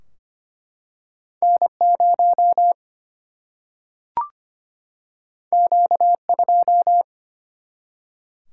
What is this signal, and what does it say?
Morse code, heard as a keyed tone, about 25 wpm, 700 Hz. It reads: N0 Q2
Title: Single-Letter Single-Number 25wpm